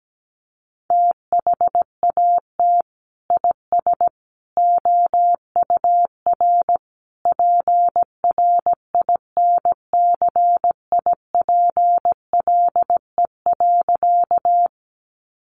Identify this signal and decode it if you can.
Morse code, heard as a keyed tone, about 17 wpm, 700 Hz. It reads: THAT IS OUR PRINCIPLE.
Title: Sentences from Top 1400 Words 17wpm